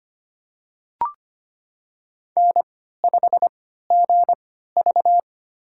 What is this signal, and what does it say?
Morse code, heard as a keyed tone, about 25 wpm, spaced as for 14 wpm, 700 Hz. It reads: N5GV
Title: USA Call Signs - 4 Characters - 3x Character Spacing - 25wpm